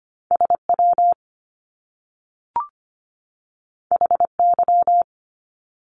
Morse code, heard as a keyed tone, about 25 wpm, 700 Hz. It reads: SW HY
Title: Top Two Letter Combinations in Words 25wpm